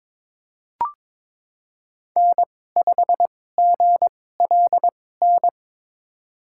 Morse code, heard as a keyed tone, about 22 wpm, spaced as for 16 wpm, 700 Hz. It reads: N5GLN
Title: USA Call Signs - 5 Characters - 2x Character Spacing - 22wpm